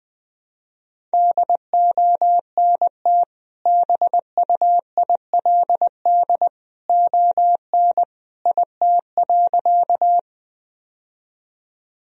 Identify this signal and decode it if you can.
Morse code, heard as a keyed tone, about 20 wpm, 700 Hz. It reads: DONT BUILD ON IT.
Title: Sentences from Top 1000 Words - Review Part 1 - 20wpm